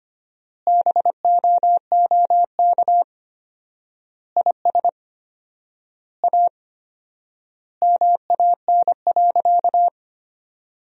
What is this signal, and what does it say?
Morse code, heard as a keyed tone, about 25 wpm, 700 Hz. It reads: BOOK IS A MAN.
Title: 4x Word-Spacing - Sentences from Top 500 Words - 25wpm